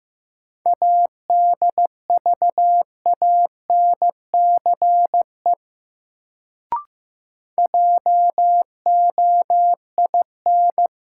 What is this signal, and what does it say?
Morse code, heard as a keyed tone, about 15 wpm, 700 Hz. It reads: ADVANCE JOIN
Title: Sentences from Top 1400 Words - New Word Drill Alt - 15wpm